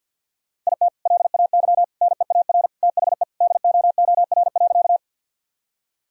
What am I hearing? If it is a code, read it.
Morse code, heard as a keyed tone, about 50 wpm, 700 Hz. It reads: IT LAY NEAR THE DOOR.
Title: Sentences from Top 1000 Words - Review Part 2 - 50wpm